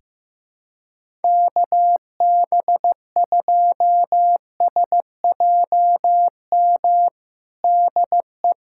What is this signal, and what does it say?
Morse code, heard as a keyed tone, about 15 wpm, 700 Hz. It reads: KB2SJM DE
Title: USA Callsign DE Callsign - 6 Characters - 15wpm